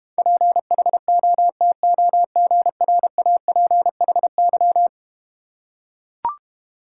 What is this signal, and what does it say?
Morse code, heard as a keyed tone, about 31 wpm, 700 Hz. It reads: PHOTOGRAPHY
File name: Easy Words - Speed-Racing with Hints - 25wpm